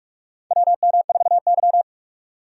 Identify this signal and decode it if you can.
Morse code, heard as a keyed tone, about 46 wpm, 700 Hz. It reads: WM4Y
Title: USA Call Signs - 4 Characters - 6x Character Spacing - 45wpm